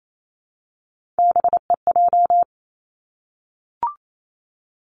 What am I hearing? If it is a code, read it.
Morse code, heard as a keyed tone, about 28 wpm, 700 Hz. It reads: BEJ
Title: Three Challenging Letter-Number Combinations 28wpm